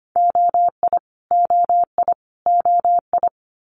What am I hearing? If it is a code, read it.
Morse code, heard as a keyed tone, about 25 wpm, 700 Hz. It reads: OI OI OI
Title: Multi-letter Phonemes Repeated 3x 25wpm